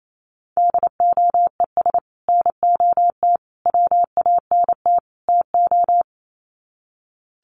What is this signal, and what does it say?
Morse code, heard as a keyed tone, about 28 wpm, 700 Hz. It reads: DOES NOT WANT TO
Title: Sets of 4 Words 28wpm